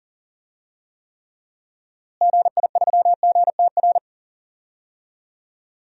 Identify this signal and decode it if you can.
Morse code, heard as a keyed tone, about 40 wpm, 700 Hz. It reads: GI3GTR
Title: DXCC-100 Call Signs - 6 Characters - Repeated 2x - 40wpm